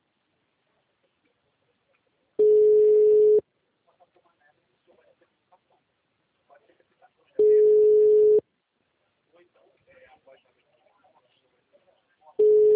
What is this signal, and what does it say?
Ringback tone (Europe, 425 Hz), with speech.
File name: out- -2016- - - .17835